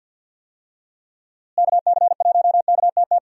Morse code, heard as a keyed tone, about 50 wpm, 700 Hz. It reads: KC1KTT
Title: USA Call Signs - 6 Characters - Repeated 3x - 50wpm